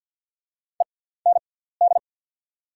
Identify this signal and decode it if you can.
Morse code, heard as a keyed tone, about 48 wpm, spaced as for 17 wpm, 700 Hz. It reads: END
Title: 6x Character Spacing - Top 200 Words - 50wpm